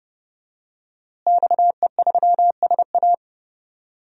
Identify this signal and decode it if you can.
Morse code, heard as a keyed tone, about 30 wpm, 700 Hz. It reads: XE3SA
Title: DXCC-100 Call Signs - 5 Characters - Repeated 2x - 30wpm